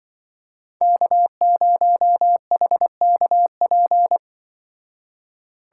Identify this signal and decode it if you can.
Morse code, heard as a keyed tone, about 24 wpm, 700 Hz. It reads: K0HKP
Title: Speed Racing - USA Call Signs 20wpm